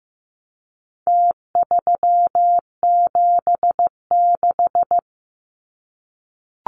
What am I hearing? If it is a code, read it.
Morse code, heard as a keyed tone, about 15 wpm, 700 Hz. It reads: T376
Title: Numbers - Contest Style - 4-Digit 15wpm